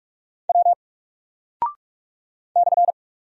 Morse code, heard as a keyed tone, about 46 wpm, 700 Hz. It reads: W /
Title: Single Character (Letter:Number:Punctuation:Prosign) 45wpm